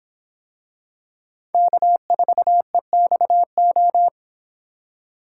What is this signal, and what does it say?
Morse code, heard as a keyed tone, about 23 wpm, 700 Hz. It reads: K4EXO